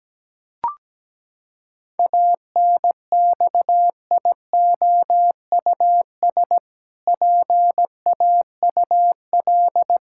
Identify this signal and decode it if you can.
Morse code, heard as a keyed tone, about 17 wpm, 700 Hz. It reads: ANXIOUS PAUL
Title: Pangram Challenge v2 - 17wpm